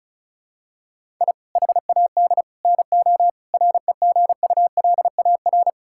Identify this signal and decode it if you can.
Morse code, heard as a keyed tone, about 36 wpm, 700 Hz. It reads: I HAD NO REGULAR
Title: Sentences from Top 1000 Words - Review Part 6 - 35wpm